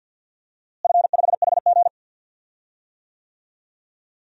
Morse code, heard as a keyed tone, about 50 wpm, 700 Hz. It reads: F5HG
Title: DXCC-100 Call Signs - 4 Characters - Repeated 2x - 50wpm